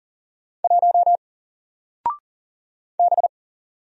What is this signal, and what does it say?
Morse code, heard as a keyed tone, about 40 wpm, 700 Hz. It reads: 1 B
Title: Single Letter-Number Alt 40wpm